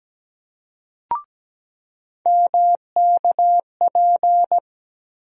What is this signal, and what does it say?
Morse code, heard as a keyed tone, about 17 wpm, 700 Hz. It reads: MKP
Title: USA Call Sign Postfixes - 3 Letters 17wpm